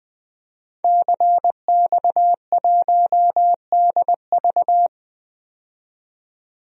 Morse code, heard as a keyed tone, about 20 wpm, 700 Hz. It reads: CX1DV
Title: DXCC-100 Call Signs - 5 Characters - 20wpm